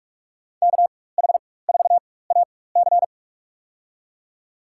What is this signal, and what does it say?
Morse code, heard as a keyed tone, about 45 wpm, spaced as for 21 wpm, 700 Hz. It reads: KH4AC
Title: USA Call Signs - 4x Character Spacing - 45wpm